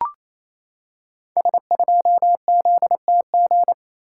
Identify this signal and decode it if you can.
Morse code, heard as a keyed tone, about 28 wpm, 700 Hz. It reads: S2ZTG
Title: Code Groups - 5 Letter:Numbers - 28wpm